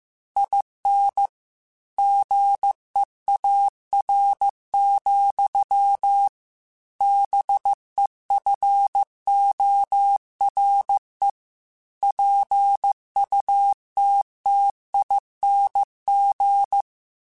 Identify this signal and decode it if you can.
Morse code, heard as a keyed tone, about 15 wpm, 800 Hz. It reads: IN GEAR, BEFORE PUTTING